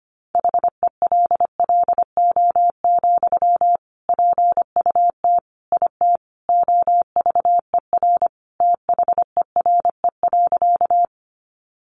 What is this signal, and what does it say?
Morse code, heard as a keyed tone, about 25 wpm, 700 Hz. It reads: HELLO, PUT IT OVER THERE.
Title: Sentences from Top 1100 Words 25wpm